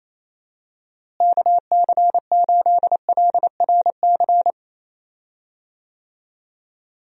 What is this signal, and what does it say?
Morse code, heard as a keyed tone, about 28 wpm, 700 Hz. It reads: KC8LRC
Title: USA Call Signs - 6 Characters - Repeated 3x - 28wpm